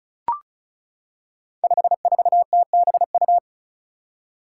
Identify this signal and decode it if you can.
Morse code, heard as a keyed tone, about 36 wpm, 700 Hz. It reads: 54TBU
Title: Five Challenging Letter-Number Combinations 35wpm